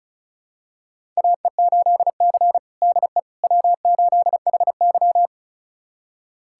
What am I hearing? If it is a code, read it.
Morse code, heard as a keyed tone, about 36 wpm, 700 Hz. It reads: AE8C DE W8HY
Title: USA Callsign DE Callsign - 4 Characters - 35wpm